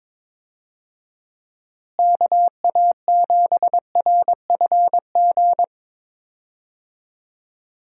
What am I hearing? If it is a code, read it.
Morse code, heard as a keyed tone, about 22 wpm, 700 Hz. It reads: KA7RFG
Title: USA Call Signs - 6 Characters - Repeated 2x - 22wpm